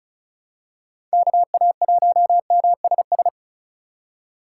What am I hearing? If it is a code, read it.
Morse code, heard as a keyed tone, about 36 wpm, 700 Hz. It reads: KA1MSS
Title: USA Call Signs - 8x Character Spacing - 35wpm